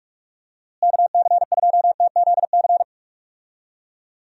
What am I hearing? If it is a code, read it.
Morse code, heard as a keyed tone, about 41 wpm, 700 Hz. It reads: KC2TZC
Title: USA Call Signs - 6 Characters - Repeated 2x - 45wpm